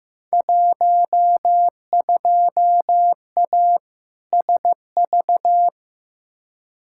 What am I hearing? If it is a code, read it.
Morse code, heard as a keyed tone, about 15 wpm, 700 Hz. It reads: 12A SV
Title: Field Day Exchanges Encore - 15wpm